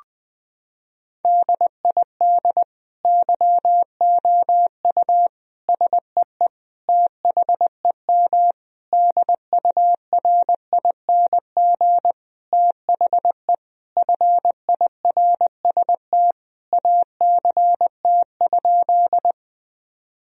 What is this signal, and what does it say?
Morse code, heard as a keyed tone, about 21 wpm, 700 Hz. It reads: DID YOU SEE THEM DURING THE FIRST ACT?
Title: Sentences from Top 1000 Words - Review Part 2 - 20wpm